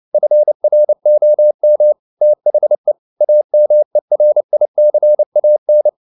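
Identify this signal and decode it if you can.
Morse code, heard as a keyed tone, about 29 wpm, 600 Hz. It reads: FROM THE AMERICAN
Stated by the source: Morse code at 29 wpm, sending Ken Follett's novel Never in English, from CW med Gnister.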